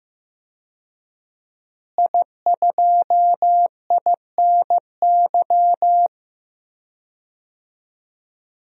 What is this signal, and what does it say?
Morse code, heard as a keyed tone, about 15 wpm, 700 Hz. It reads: I2INY
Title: DXCC-100 Call Signs - 5 Characters - Repeated 3x - 15wpm